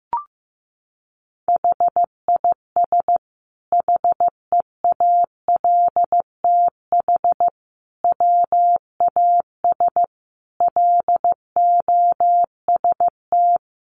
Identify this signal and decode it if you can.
Morse code, heard as a keyed tone, about 15 wpm, 700 Hz. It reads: HIS HEALTH WAS LOST
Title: Sentences from Top 1400 Words 15wpm